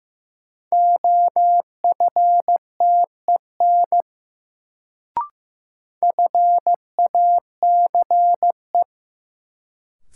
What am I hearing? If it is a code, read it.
Morse code, heard as a keyed tone, about 15 wpm, 700 Hz. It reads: OFTEN FACE
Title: Sentences from Top 300 Words - New Word Drill - 15wpm